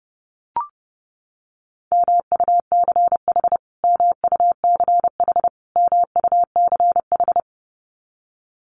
Morse code, heard as a keyed tone, about 30 wpm, 700 Hz. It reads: MUCH MUCH MUCH
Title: Top 300 Words Repeated 30wpm